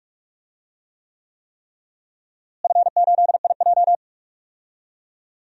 Morse code, heard as a keyed tone, about 46 wpm, 700 Hz. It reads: F8IJ
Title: DXCC-100 Call Signs - 4 Characters - Repeated 3x - 45wpm